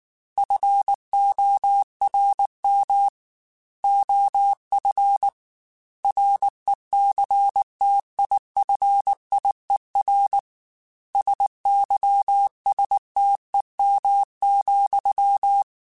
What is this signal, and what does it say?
Morse code, heard as a keyed tone, about 19 wpm, 800 Hz. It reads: FORM OF RECTIFIER SYSTEM,
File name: Technical Subjects / Technical Subjects-mp3s / Sw Pwr Sup @18